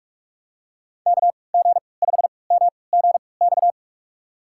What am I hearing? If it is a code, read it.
Morse code, heard as a keyed tone, about 46 wpm, spaced as for 26 wpm, 700 Hz. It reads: KG5MGX
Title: USA Call Signs - 6 Characters - 3x Character Spacing - 45wpm